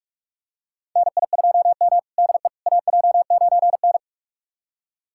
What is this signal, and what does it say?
Morse code, heard as a keyed tone, about 41 wpm, 700 Hz. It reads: NI2M DE AJ9N
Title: USA Callsign DE Callsign - 4 Characters - 45wpm